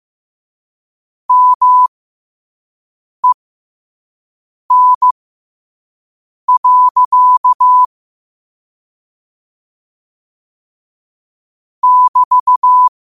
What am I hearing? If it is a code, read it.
Morse code, heard as a keyed tone, about 15 wpm, spaced as for 5 wpm, 1000 Hz. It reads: MEN. <BT>
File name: qotdfiles / 5wpm